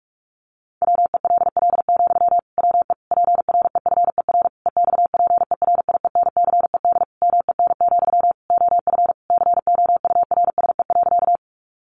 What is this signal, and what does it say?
Morse code, heard as a keyed tone, about 44 wpm, 700 Hz. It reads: WELL, WE PREFER EXPERIENCED MEN, OF COURSE.
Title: Sentences from Top 1100 Words 45wpm